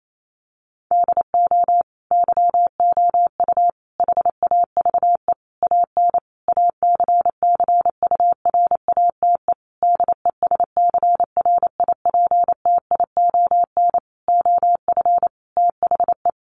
Morse code, heard as a keyed tone, about 28 wpm, 700 Hz. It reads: DO YOU HAVE AN ACCURATE DESCRIPTION OF THE
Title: Sentences from Top 1300 Words 28wpm